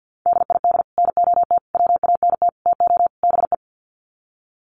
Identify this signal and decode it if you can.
Morse code, heard as a keyed tone, about 50 wpm, 700 Hz. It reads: DID NOT WANT TO BE
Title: Sets of 5 Words 50wpm